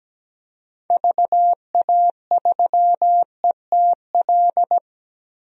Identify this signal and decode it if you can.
Morse code, heard as a keyed tone, about 17 wpm, 700 Hz. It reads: VA3ETL
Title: DXCC-100 Call Signs - 6 Characters - 17wpm